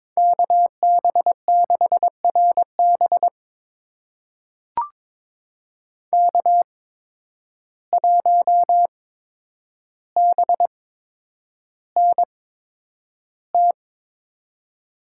Morse code, heard as a keyed tone, about 22 wpm, spaced as for 9 wpm, 700 Hz. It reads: KB6RB K1BNT
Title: USA Call Signs - 5 Characters - 8x Character Spacing - 22wpm